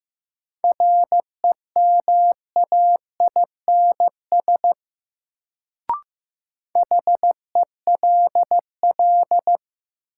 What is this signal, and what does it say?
Morse code, heard as a keyed tone, about 15 wpm, 700 Hz. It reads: REMAINS HELL
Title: Sets of 5 Words Encore - New Word Drill - 15wpm